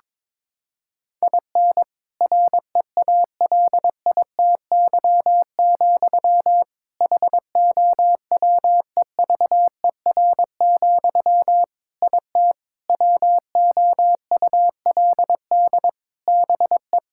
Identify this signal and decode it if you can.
Morse code, heard as a keyed tone, about 22 wpm, 700 Hz. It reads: IN REALITY, HOWEVER, IT WOULD BE
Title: Sentences from Top 1600 Words 22wpm